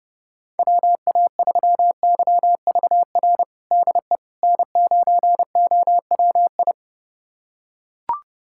Morse code, heard as a keyed tone, about 30 wpm, 700 Hz. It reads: WA3YVR DE N9OWI
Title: USA Callsign DE Callsign 30wpm